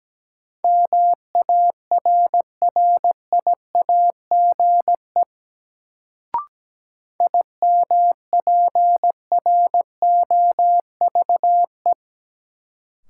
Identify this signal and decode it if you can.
Morse code, heard as a keyed tone, about 17 wpm, 700 Hz. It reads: MARRIAGE IMPROVE
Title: Sentences from Top 800 Words - New Word Drill - 17wpm